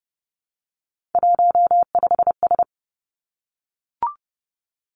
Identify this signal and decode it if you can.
Morse code, heard as a keyed tone, about 30 wpm, 700 Hz. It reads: 15S
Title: Three Challenging Letter-Number Combinations 30wpm